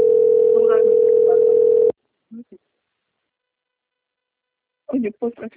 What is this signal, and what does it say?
Ringback tone (North America), 440 + 480 Hz, with speech.
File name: out- - - - - .36513